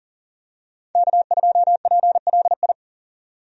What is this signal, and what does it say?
Morse code, heard as a keyed tone, about 37 wpm, 700 Hz. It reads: K2PLI
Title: USA Call Signs - 5 Characters - 40wpm